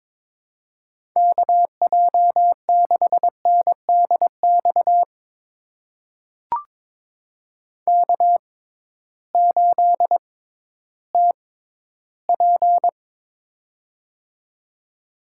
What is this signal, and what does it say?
Morse code, heard as a keyed tone, about 22 wpm, spaced as for 13 wpm, 700 Hz. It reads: KJ6NDX K8TP
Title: USA Call Signs - 6x Character Spacing - 22wpm